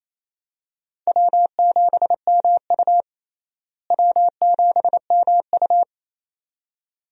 Morse code, heard as a keyed tone, about 28 wpm, 700 Hz. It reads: W7MU W7MU
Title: USA Call Signs - 4 Characters - Repeated 2x - 28wpm